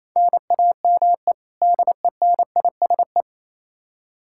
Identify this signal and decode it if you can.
Morse code, heard as a keyed tone, about 28 wpm, 700 Hz. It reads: NAME DENISE
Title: Practice QSO - Exchange 1 - Beginning - 28wpm